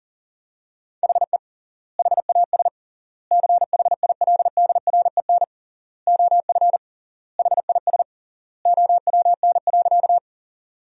Morse code, heard as a keyed tone, about 40 wpm, 700 Hz. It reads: HE HAS CHILDREN OF HIS OWN.
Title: 3x Word-Spacing - Sentences from Top 300 Words - 40wpm